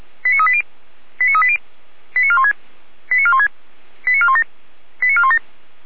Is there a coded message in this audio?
Yes: five-tone selective call (ZVEI) from 0.2 to 5.4 s, heard as quick runs of five tones, tones of about 70 ms. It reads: 89380 89380 89416 89416 89417 89417